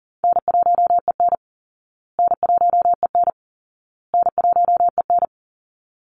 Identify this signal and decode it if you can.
Morse code, heard as a keyed tone, about 40 wpm, 700 Hz. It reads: N1EN N1EN N1EN
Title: Call Signs Repeated 2x - 40wpm